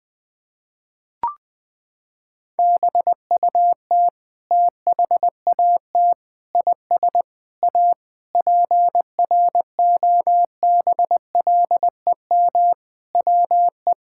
Morse code, heard as a keyed tone, about 21 wpm, 700 Hz. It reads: BUT THAT IS A PROBLEM WE
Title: Sentences from Top 1000 Words - Review Part 6 - 20wpm